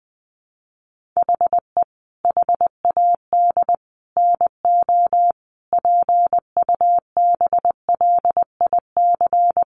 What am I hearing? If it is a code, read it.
Morse code, heard as a keyed tone, about 19 wpm, 700 Hz. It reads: HE HAD NO PUBLIC
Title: Sentences from Top 1500 Words 20wpm